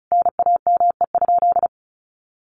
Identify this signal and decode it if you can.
Morse code, heard as a keyed tone, about 34 wpm, 700 Hz. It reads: NAME?
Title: Sentences from Top 600 Words 35wpm